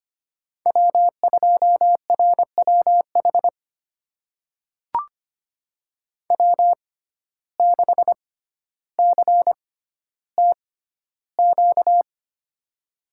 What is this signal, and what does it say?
Morse code, heard as a keyed tone, about 25 wpm, spaced as for 13 wpm, 700 Hz. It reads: W2RWH W6CTQ
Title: USA Call Signs - 5 Characters - 6x Character Spacing - 25wpm